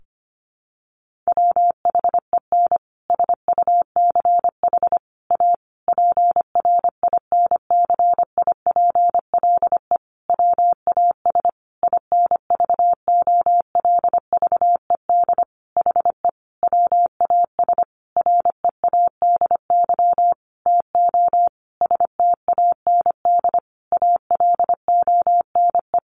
Morse code, heard as a keyed tone, about 25 wpm, 700 Hz. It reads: WHEN SUCH A PRINCIPLE WAS INVOLVED HE WAS READY TO STAND ALONE
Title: Sentences from Top 1400 Words 25wpm